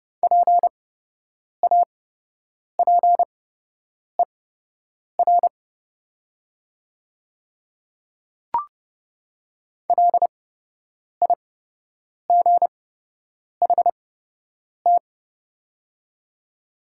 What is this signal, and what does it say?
Morse code, heard as a keyed tone, about 30 wpm, spaced as for 8 wpm, 700 Hz. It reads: PAPER LIGHT
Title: 8x Character Spacing - Top 300 Words - 30wpm